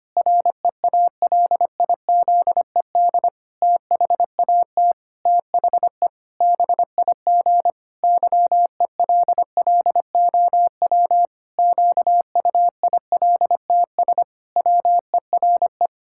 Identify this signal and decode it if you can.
Morse code, heard as a keyed tone, about 25 wpm, 700 Hz. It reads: REALIZED THAT THE BIG YELLOW QUILTS WERE
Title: Pangram Challenge v2 - 25wpm